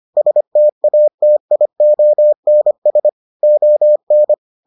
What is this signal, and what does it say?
Morse code, heard as a keyed tone, about 25 wpm, 600 Hz. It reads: STATIONS ON